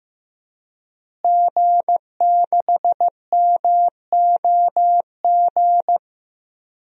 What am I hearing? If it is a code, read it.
Morse code, heard as a keyed tone, about 15 wpm, 700 Hz. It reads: G6MOG
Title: DXCC-100 Call Signs - 5 Characters - Repeated 3x - 15wpm